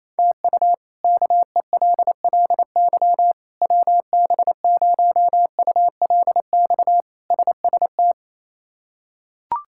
Morse code, heard as a keyed tone, about 27 wpm, 700 Hz. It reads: TU KELLY WB0ULX SST
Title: K1USN SST QSO Practice - 28wpm